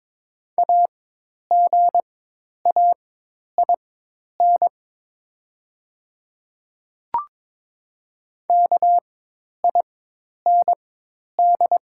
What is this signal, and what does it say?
Morse code, heard as a keyed tone, about 22 wpm, spaced as for 10 wpm, 700 Hz. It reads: AGAIN KIND
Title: 4x Character Spacing - Top 200 Words - 22wpm